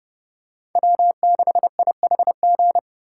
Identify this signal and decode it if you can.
Morse code, heard as a keyed tone, about 29 wpm, 700 Hz. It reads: W6IHG
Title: K1USN SST QSO Practice - 30wpm